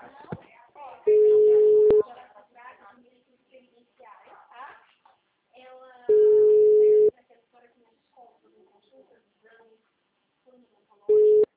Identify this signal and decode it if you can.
Ringback tone (Europe, 425 Hz), with speech.